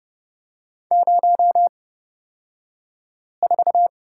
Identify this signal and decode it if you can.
Morse code, heard as a keyed tone, about 29 wpm, spaced as for 5 wpm, 700 Hz. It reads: 04
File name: Lesson 38d - Letter q - Two Characters 30wpm